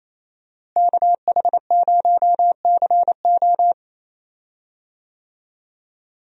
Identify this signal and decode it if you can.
Morse code, heard as a keyed tone, about 28 wpm, 700 Hz. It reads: KH0CO
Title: USA Call Signs - 5 Characters - Repeated 2x - 28wpm